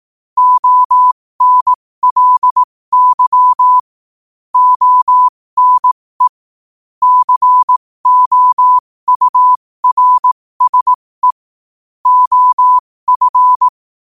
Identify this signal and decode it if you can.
Morse code, heard as a keyed tone, about 18 wpm, spaced as for 15 wpm, 1000 Hz. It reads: ONLY ONE COURSE OF
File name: qotdfiles / 15wpm